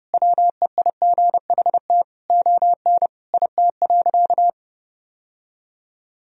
Morse code, heard as a keyed tone, about 32 wpm, 700 Hz. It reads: WEIGHT ON IT.
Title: Sentences from Top 1000 Words - Review Part 3 - 30wpm